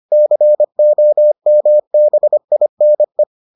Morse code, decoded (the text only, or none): COMBINE